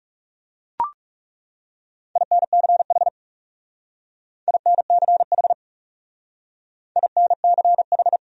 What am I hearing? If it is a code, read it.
Morse code, heard as a keyed tone, about 38 wpm, 700 Hz. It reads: INCH INCH INCH